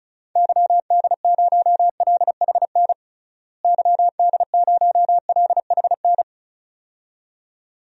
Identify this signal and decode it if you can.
Morse code, heard as a keyed tone, about 34 wpm, 700 Hz. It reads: YD0LHN YD0LHN
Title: DXCC-100 Call Signs - 6 Characters - Repeated 2x - 35wpm